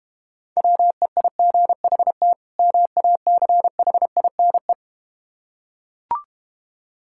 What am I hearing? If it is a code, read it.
Morse code, heard as a keyed tone, about 31 wpm, 700 Hz. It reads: WEIGHT MACHINE
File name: practice / 25wpm / Difficult Words - Speed-Racing with Hints - 25wpm